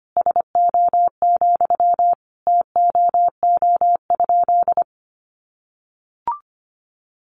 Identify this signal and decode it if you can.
Morse code, heard as a keyed tone, about 25 wpm, 700 Hz. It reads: SO, TOO?
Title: Sentences from Top 200 Words 25wpm